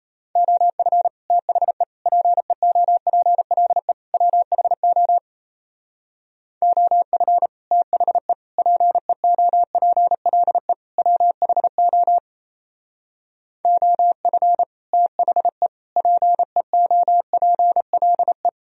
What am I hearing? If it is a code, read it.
Morse code, heard as a keyed tone, about 35 wpm, 700 Hz. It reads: OF THE PEOPLE WHO OF THE PEOPLE WHO OF THE PEOPLE
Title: Speed-Racing - Sets of 4 Words 28wpm